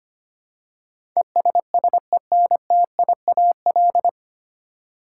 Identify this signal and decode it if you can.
Morse code, heard as a keyed tone, about 25 wpm, 700 Hz. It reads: ESSENTIAL
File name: Nine Letter Words - 25wpm